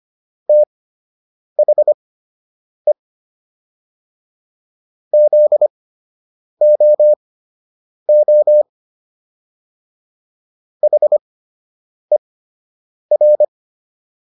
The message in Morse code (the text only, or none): THE ZOO HER